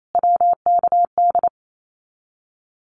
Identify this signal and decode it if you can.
Morse code, heard as a keyed tone, about 28 wpm, 700 Hz. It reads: WKD
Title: Abbreviations Repeated 3x - 28wpm